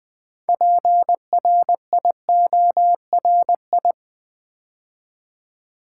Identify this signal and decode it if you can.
Morse code, heard as a keyed tone, about 20 wpm, 700 Hz. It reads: PRIORI
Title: Build A Word - 2 Letters At A Time - 8-Letter Words - 20wpm